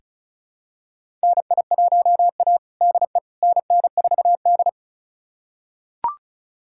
Morse code, heard as a keyed tone, about 36 wpm, 700 Hz. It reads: NI1A DE NN4D
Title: USA Callsign DE Callsign - 4 Characters - 35wpm